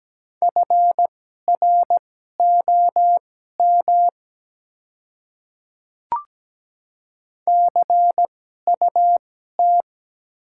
Morse code, heard as a keyed tone, about 17 wpm, spaced as for 12 wpm, 700 Hz. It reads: FROM CUT
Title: 2x Character Spacing - Top 300 Words - 17wpm